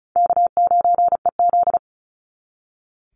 Morse code, heard as a keyed tone, about 36 wpm, 700 Hz. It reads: K9EZ
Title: NA SOTA Chaser Callsigns 35wpm